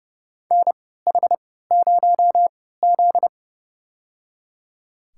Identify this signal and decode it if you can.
Morse code, heard as a keyed tone, about 29 wpm, spaced as for 17 wpm, 700 Hz. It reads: NH0Z
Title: USA Call Signs - 3x Character Spacing - 30wpm